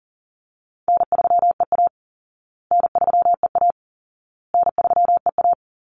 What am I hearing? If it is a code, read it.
Morse code, heard as a keyed tone, about 37 wpm, 700 Hz. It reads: N3EA N3EA N3EA
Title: Easy Call Signs Repeated 3x - 40wpm